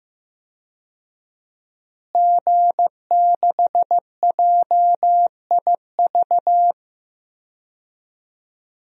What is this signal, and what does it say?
Morse code, heard as a keyed tone, about 15 wpm, 700 Hz. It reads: G6JIV
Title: DXCC-100 Call Signs - 5 Characters - Repeated 3x - 15wpm